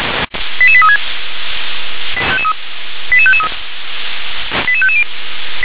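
Five-tone selective call (ZVEI), heard as quick runs of five tones, tones of about 70 ms. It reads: 88836 88553 88550
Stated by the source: ZVEI five-tone selective call, code 88702, received via SDR.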